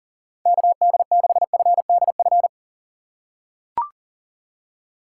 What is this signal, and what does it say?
Morse code, heard as a keyed tone, about 37 wpm, 700 Hz. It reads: KD6FDF
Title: USA Call Signs - 4x Character Spacing - 40wpm